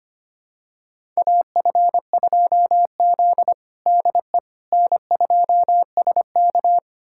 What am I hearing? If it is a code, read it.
Morse code, heard as a keyed tone, about 25 wpm, 700 Hz. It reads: AF2Z DE N2SK
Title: USA Callsign DE Callsign - 4 Characters - 25wpm